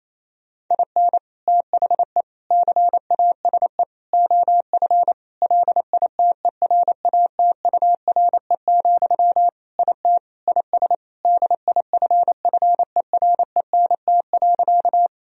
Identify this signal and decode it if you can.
Morse code, heard as a keyed tone, about 28 wpm, 700 Hz. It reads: IN THE CASE OF LITERATURE, IT IS DIFFERENT.
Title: Sentences from Top 1000 Words - Review Part 3 - 28wpm